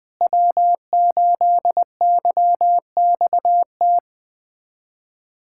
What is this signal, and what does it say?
Morse code, heard as a keyed tone, about 21 wpm, 700 Hz. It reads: W8YXT
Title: USA Callsign DE Callsign - 5 Characters - 20wpm